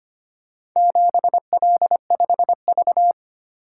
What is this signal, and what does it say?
Morse code, heard as a keyed tone, about 25 wpm, 700 Hz. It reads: 7L5V
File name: practice / 25wpm / Four Letter-Number Combinations - 25wpm